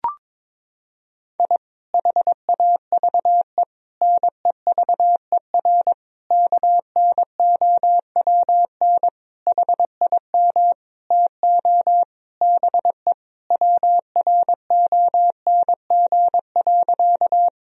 Morse code, heard as a keyed tone, about 22 wpm, 700 Hz. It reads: I HAVE NEVER KNOWN HIM TO BE WRONG.
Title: Sentences from Top 1000 Words - Review Part 3 - 22wpm